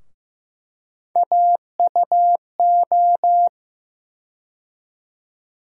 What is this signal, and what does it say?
Morse code, heard as a keyed tone, about 15 wpm, 700 Hz. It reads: AUO
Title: USA Call Sign Postfixes - 3 Letters 15wpm